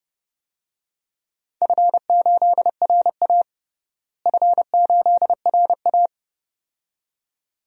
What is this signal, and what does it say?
Morse code, heard as a keyed tone, about 31 wpm, 700 Hz. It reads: F8RA F8RA
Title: DXCC-100 Call Signs - 4 Characters - Repeated 2x - 30wpm